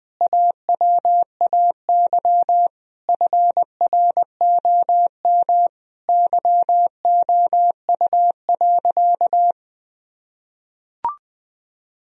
Morse code, heard as a keyed tone, about 20 wpm, 700 Hz. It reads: AWAY FROM YOU.
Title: Sentences from Top 1000 Words - Review Part 1 - 20wpm